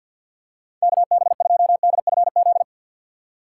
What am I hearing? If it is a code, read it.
Morse code, heard as a keyed tone, about 52 wpm, 700 Hz. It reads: KB2DFZ
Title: USA Call Signs - 6 Characters - 6x Character Spacing - 50wpm